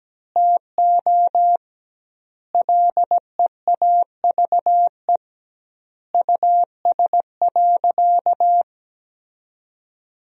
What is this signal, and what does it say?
Morse code, heard as a keyed tone, about 17 wpm, 700 Hz. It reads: TO LEAVE US.
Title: 2x Word-Spacing - Sentences from Top 400 Words - 17wpm